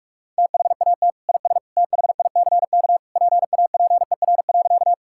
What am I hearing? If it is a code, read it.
Morse code, heard as a keyed tone, about 46 wpm, 700 Hz. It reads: THAT IS THICK PAPER.
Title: Sentences from Top 1600 Words 45wpm